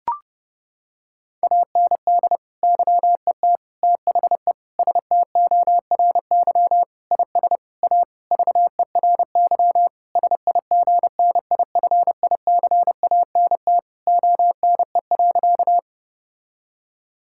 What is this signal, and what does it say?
Morse code, heard as a keyed tone, about 30 wpm, 700 Hz. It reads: AND YET THE STORY IS A VERY SIGNIFICANT ONE.
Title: Sentences from Top 1000 Words - Review Part 6 - 30wpm